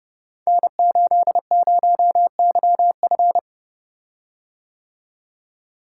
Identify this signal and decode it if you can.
Morse code, heard as a keyed tone, about 29 wpm, 700 Hz. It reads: N80YF
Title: Code Groups - 5 Letter:Numbers - 30wpm